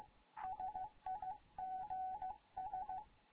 Morse code, heard as a keyed tone, about 15 wpm, 750 Hz. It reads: SIGS
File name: Morze-HA2MN / hangp / 2 11 GYAKORLAT